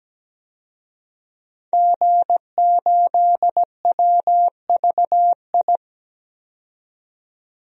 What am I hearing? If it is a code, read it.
Morse code, heard as a keyed tone, about 17 wpm, 700 Hz. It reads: G8WVI
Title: DXCC-100 Call Signs - 5 Characters - 17wpm